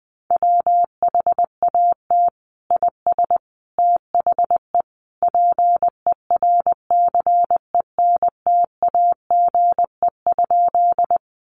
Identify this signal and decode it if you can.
Morse code, heard as a keyed tone, about 20 wpm, 700 Hz. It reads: WHAT IS THE PERCENTAGE?
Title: Sentences from Top 1300 Words 20wpm